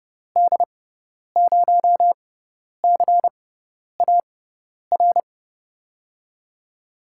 Morse code, heard as a keyed tone, about 30 wpm, spaced as for 10 wpm, 700 Hz. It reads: D0CAR
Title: USA Call Signs - 6 Characters - 6x Character Spacing - 30wpm